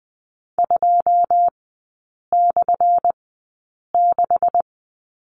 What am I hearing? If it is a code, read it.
Morse code, heard as a keyed tone, about 20 wpm, spaced as for 8 wpm, 700 Hz. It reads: 2/6
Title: 184 Lesson 30e - Letter b - Three Characters 20wpm